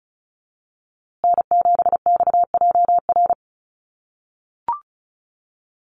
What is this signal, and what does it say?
Morse code, heard as a keyed tone, about 34 wpm, 700 Hz. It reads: N7XJR
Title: USA Call Signs Repeated 3x 35wpm